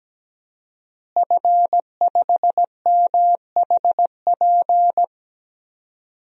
Morse code, heard as a keyed tone, about 17 wpm, 700 Hz. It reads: F5MHP
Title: DXCC-100 Call Signs - 5 Characters - Repeated 2x - 17wpm